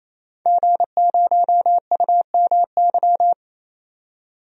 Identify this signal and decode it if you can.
Morse code, heard as a keyed tone, about 28 wpm, 700 Hz. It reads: G0UMY
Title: DXCC-100 Call Signs - 5 Characters - 28wpm